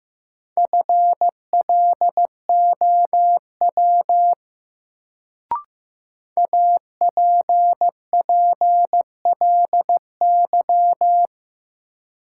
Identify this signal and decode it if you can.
Morse code, heard as a keyed tone, about 15 wpm, 700 Hz. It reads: FLOW APPLY
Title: Sentences from Top 1200 Words - New Word Drill - 15wpm